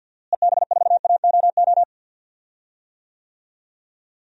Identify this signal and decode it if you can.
Morse code, heard as a keyed tone, about 52 wpm, 700 Hz. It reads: EB4AOO
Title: DXCC-100 Call Signs - 6 Characters - Repeated 3x - 50wpm